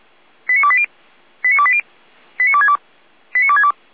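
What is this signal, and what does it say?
Five-tone selective call (ZVEI), heard as quick runs of five tones, tones of about 70 ms. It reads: 89280 89280 89262 89262
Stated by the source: ZVEI five-tone call 89319 received by SDR, logged by Feuerwehr Orlamünde.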